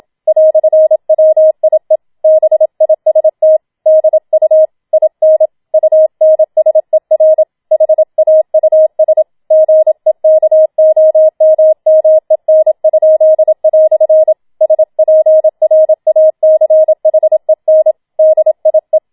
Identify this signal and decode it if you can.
Morse code, heard as a keyed tone, about 27 wpm, 630 Hz. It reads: "WIE BIST DU IN UNSER HAUS GEKOMMEN?" SPRACHEN DIE
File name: Schneewittchen Teil2 27wpm